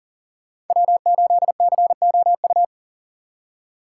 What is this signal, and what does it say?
Morse code, heard as a keyed tone, about 52 wpm, 700 Hz. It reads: W8COU